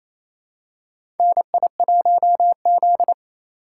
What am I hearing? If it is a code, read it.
Morse code, heard as a keyed tone, about 28 wpm, 700 Hz. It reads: NI1Z
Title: USA Call Signs - 4x Character Spacing - 28wpm